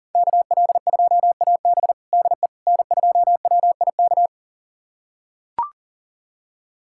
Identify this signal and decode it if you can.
Morse code, heard as a keyed tone, about 40 wpm, 700 Hz. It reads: KL2AB DE N2WIK
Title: USA Callsign DE Callsign - 5 Characters - 40wpm